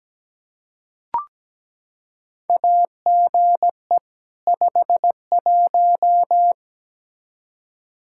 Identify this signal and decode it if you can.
Morse code, heard as a keyed tone, about 17 wpm, 700 Hz. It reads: AGE 51
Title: Practice QSO - Exchange 3 - More Info - 17wpm